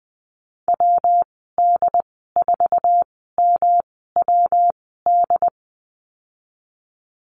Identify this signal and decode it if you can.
Morse code, heard as a keyed tone, about 20 wpm, spaced as for 14 wpm, 700 Hz. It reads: WD4MWD
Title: 154 Lesson 25g - Letter p - Call Signs 20wpm